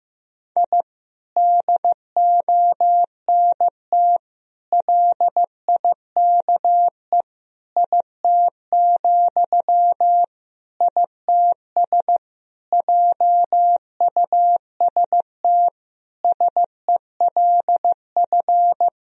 Morse code, heard as a keyed tone, about 16 wpm, 700 Hz. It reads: I DONT LIKE IT, ITS JUST SELF
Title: Sentences from Top 1000 Words - Review Part 1 - 15wpm